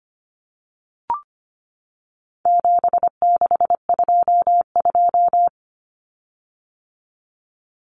Morse code, heard as a keyed tone, about 25 wpm, 700 Hz. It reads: 7622
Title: Common Cut Numbers - 4-Digit 25wpm